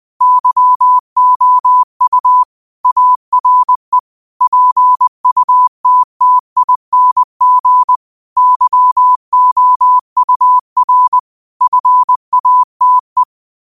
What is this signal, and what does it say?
Morse code, heard as a keyed tone, about 20 wpm, 1000 Hz. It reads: YOU ARE PUTTING YOUR FATE
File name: qotdfiles / 20wpm